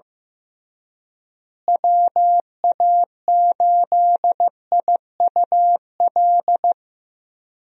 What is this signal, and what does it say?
Morse code, heard as a keyed tone, about 15 wpm, 700 Hz. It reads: WA8IUL
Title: USA Call Signs - 6 Characters - Repeated 3x - 15wpm